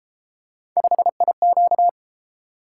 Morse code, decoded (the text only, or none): 5IQ